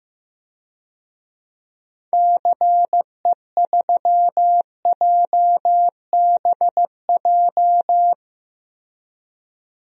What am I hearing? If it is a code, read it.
Morse code, heard as a keyed tone, about 16 wpm, 700 Hz. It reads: CE3JBJ
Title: DXCC-100 Call Signs - 6 Characters - Repeated 2x - 15wpm